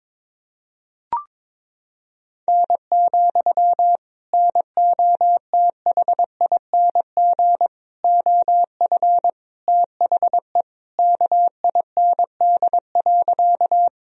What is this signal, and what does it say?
Morse code, heard as a keyed tone, about 22 wpm, 700 Hz. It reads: N, NOTHING OF THE KIND.
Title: Sentences from Top 500 Words and Abbreviations 22wpm